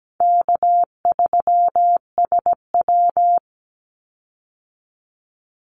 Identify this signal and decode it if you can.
Morse code, heard as a keyed tone, about 17 wpm, 700 Hz. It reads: K3SW
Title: Easy Call Signs Repeated 3x - 17wpm